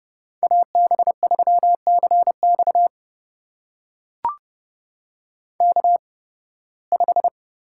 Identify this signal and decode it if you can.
Morse code, heard as a keyed tone, about 29 wpm, spaced as for 19 wpm, 700 Hz. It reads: AB3CX K5
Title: USA Call Signs - 8x Character Spacing - 30wpm